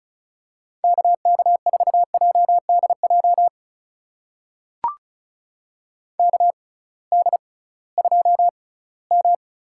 Morse code, heard as a keyed tone, about 36 wpm, spaced as for 20 wpm, 700 Hz. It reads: KK4JDJ KD2M
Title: USA Call Signs - 6 Characters - 6x Character Spacing - 35wpm